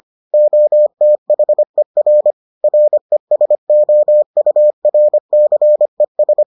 Morse code, heard as a keyed tone, about 25 wpm, 600 Hz. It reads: OTHER RESOURCES